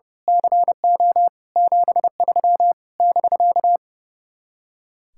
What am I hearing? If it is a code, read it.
Morse code, heard as a keyed tone, about 28 wpm, 700 Hz. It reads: CO 73 <BK>
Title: Speed-Racing Common QSO Phrases Encore - 30wpm